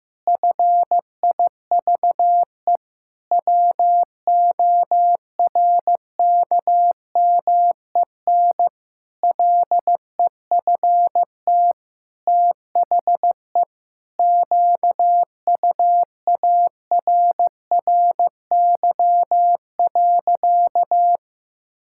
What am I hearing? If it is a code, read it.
Morse code, heard as a keyed tone, about 15 wpm, 700 Hz. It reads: FIVE WORKMEN LEFT THE QUARRY.
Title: Pangram Challenge v2 - 15wpm